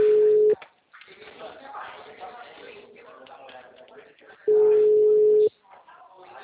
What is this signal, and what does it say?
Ringback tone (Europe, 425 Hz), with speech.
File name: out- -2016- - - .50420